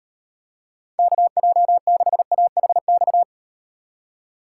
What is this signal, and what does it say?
Morse code, heard as a keyed tone, about 38 wpm, 700 Hz. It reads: KJ6AHX